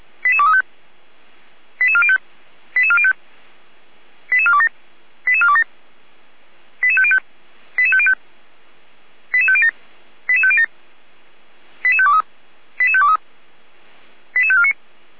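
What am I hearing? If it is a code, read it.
Five-tone selective call (ZVEI), heard as quick runs of five tones, tones of about 70 ms. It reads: 88235 88405 88405 88427 88427 88505 88505 88507 88507 88523 88523 88549
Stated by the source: Five-tone ZVEI selective call received by SDR, from the server of Feuerwehr Orlamünde.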